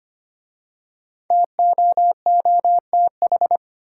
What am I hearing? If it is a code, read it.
Morse code, heard as a keyed tone, about 25 wpm, 700 Hz. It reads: TOOTH
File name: 5 Letter Words - First 3 Letters Given - 25wpm